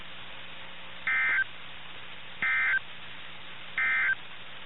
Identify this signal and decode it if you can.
Emergency Alert System (SAME) tones, heard as harsh buzzing bursts. It is heard from 1.0 to 4.2 s.
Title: NOAA Weather Radio - KIH61